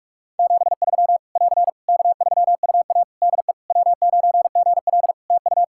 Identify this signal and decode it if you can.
Morse code, heard as a keyed tone, about 46 wpm, 700 Hz. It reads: 73 <AR> K3UA DE W9GL TU
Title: Practice QSO - Exchange 4 - Ending - 45wpm